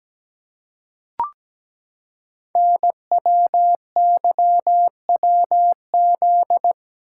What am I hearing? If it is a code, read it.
Morse code, heard as a keyed tone, about 17 wpm, 700 Hz. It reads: NWYWZ
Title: Code Groups - 5 Letters - 17wpm